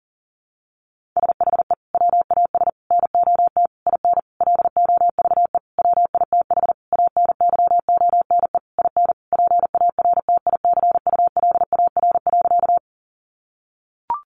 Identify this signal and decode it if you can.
Morse code, heard as a keyed tone, about 40 wpm, 700 Hz. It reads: SHE WAS NOT IN LOVE WITH ANYONE IN PARTICULAR.
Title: Sentences from Top 900 Words 40wpm